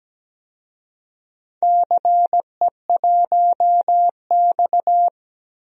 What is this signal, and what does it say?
Morse code, heard as a keyed tone, about 17 wpm, 700 Hz. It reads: CE1X
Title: DXCC-100 Call Signs - 4 Characters - Repeated 2x - 17wpm